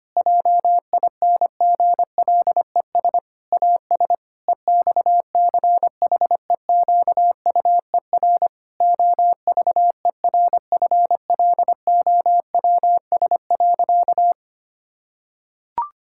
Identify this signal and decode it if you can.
Morse code, heard as a keyed tone, about 25 wpm, 700 Hz. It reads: JINGLES AS EXCHEQUER OVERFLOWS.
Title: Pangram Challenge v2 - 25wpm